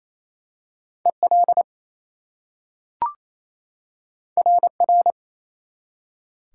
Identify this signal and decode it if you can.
Morse code, heard as a keyed tone, about 27 wpm, 700 Hz. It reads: EL RR
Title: Top 180 QSO Elements 28wpm